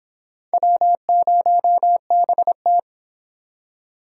Morse code, heard as a keyed tone, about 30 wpm, 700 Hz. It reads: W0BT